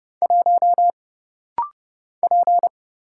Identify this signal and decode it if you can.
Morse code, heard as a keyed tone, about 30 wpm, 700 Hz. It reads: 1 P
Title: Single Letter-Number Alt - Rapid-Fire 30wpm